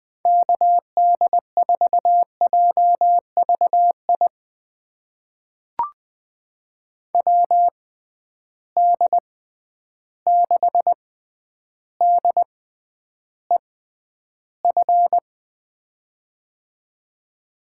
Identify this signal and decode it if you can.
Morse code, heard as a keyed tone, about 20 wpm, spaced as for 10 wpm, 700 Hz. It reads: KD4JVI WD6DEF
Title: USA Call Signs - 6 Characters - 6x Character Spacing - 20wpm